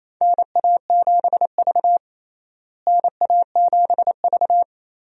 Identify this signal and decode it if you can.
Morse code, heard as a keyed tone, about 28 wpm, 700 Hz. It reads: NA7V NA7V
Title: USA Call Signs - 4 Characters - Repeated 2x - 28wpm